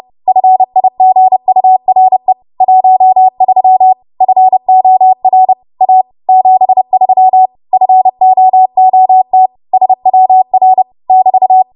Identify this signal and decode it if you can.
Morse code, heard as a keyed tone, about 31 wpm, 750 Hz. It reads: FIGURE 13 FOR A 73 FOOT SWR <BT>
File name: Morse / 30WPM